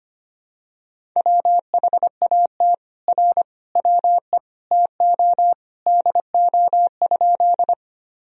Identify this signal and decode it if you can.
Morse code, heard as a keyed tone, about 25 wpm, 700 Hz. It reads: WHAT R WE TO DO?
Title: Sentences from Top 100 Words and Abbreviations 25wpm